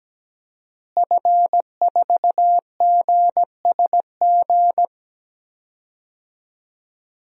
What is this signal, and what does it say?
Morse code, heard as a keyed tone, about 17 wpm, 700 Hz. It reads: F4GSG
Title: DXCC-100 Call Signs - 5 Characters - Repeated 3x - 17wpm